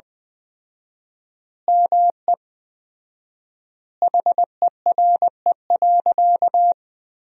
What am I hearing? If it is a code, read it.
Morse code, heard as a keyed tone, about 21 wpm, 700 Hz. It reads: ME HERE.
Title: 4x Word-Spacing - Sentences from Top 200 Words - 20wpm